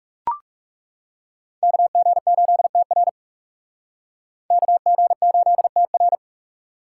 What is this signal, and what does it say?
Morse code, heard as a keyed tone, about 38 wpm, 700 Hz. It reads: KG8TR KG8TR